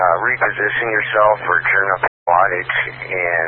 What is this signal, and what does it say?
Emergency Alert System (SAME) tones, heard as harsh buzzing bursts, with speech. It is heard from 0.2 to 3.2 s.